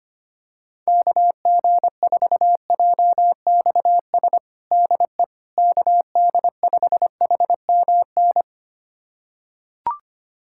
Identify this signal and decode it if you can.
Morse code, heard as a keyed tone, about 25 wpm, 700 Hz. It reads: KG4JXS DE KD5HMN
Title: USA Callsign DE Callsign - 6 Characters - 25wpm